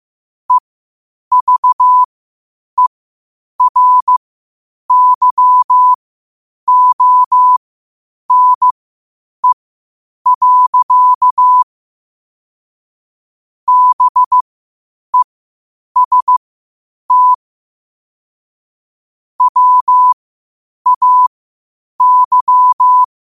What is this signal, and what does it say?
Morse code, heard as a keyed tone, about 15 wpm, spaced as for 8 wpm, 1000 Hz. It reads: EVERYONE. BEST WAY
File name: qotdfiles / 8wpm